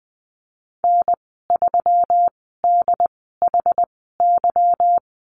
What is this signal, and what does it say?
Morse code, heard as a keyed tone, about 20 wpm, spaced as for 15 wpm, 700 Hz. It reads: N3DHY
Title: 147 Lesson 24g - Letter y - Call Signs 20wpm